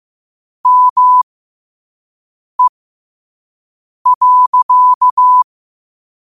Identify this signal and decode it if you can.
Morse code, heard as a keyed tone, about 15 wpm, spaced as for 5 wpm, 1000 Hz. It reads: ME.
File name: qotdfiles / 5wpm